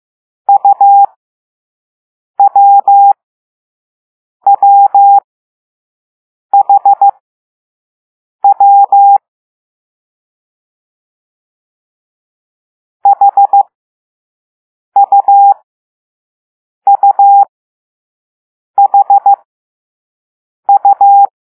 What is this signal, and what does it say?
Morse code, heard as a keyed tone, about 15 wpm, spaced as for 5 wpm, 800 Hz. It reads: UWWHW HUUHU